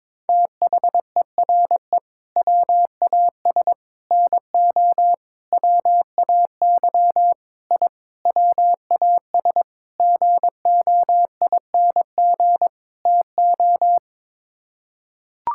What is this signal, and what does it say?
Morse code, heard as a keyed tone, about 22 wpm, 700 Hz. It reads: THERE WAS NO WAY I WAS GOING TO
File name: practice / 22wpm / Sets of 8 Words - 22wpm